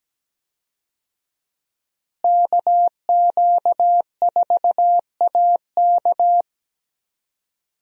Morse code, heard as a keyed tone, about 17 wpm, 700 Hz. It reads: KQ4AK
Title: USA Call Signs - 5 Characters - Repeated 2x - 17wpm